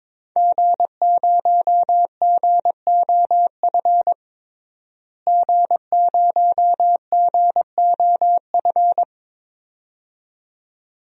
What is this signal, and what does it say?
Morse code, heard as a keyed tone, about 22 wpm, 700 Hz. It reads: G0GOF G0GOF
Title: DXCC-100 Call Signs - 5 Characters - Repeated 2x - 22wpm